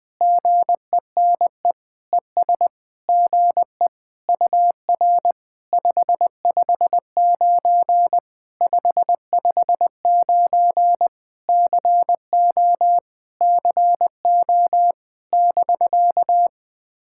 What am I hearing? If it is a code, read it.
Morse code, heard as a keyed tone, about 20 wpm, 700 Hz. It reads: GENE ES GE UR 559 559 CO CO <BK>
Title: POTA QSO Practice - 20wpm